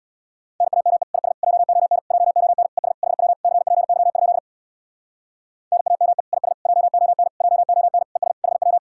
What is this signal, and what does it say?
Morse code, heard as a keyed tone, about 62 wpm, 700 Hz. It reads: NAME IS JON JON ES HW CPY? NAME IS JON JON ES HW
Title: Speed-Racing Common QSO Phrases Encore - 50wpm